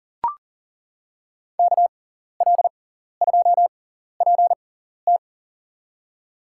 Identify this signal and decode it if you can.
Morse code, heard as a keyed tone, about 40 wpm, spaced as for 14 wpm, 700 Hz. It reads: KL2PT
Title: USA Call Signs - 5 Characters - 6x Character Spacing - 40wpm